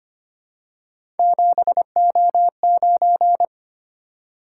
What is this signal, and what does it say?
Morse code, heard as a keyed tone, about 25 wpm, 700 Hz. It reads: 7O9
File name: Three Letter-Number Combinations - 25wpm